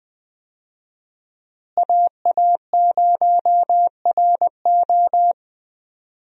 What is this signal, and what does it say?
Morse code, heard as a keyed tone, about 20 wpm, 700 Hz. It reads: AA0RO
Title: USA Call Signs - 5 Characters - Repeated 3x - 20wpm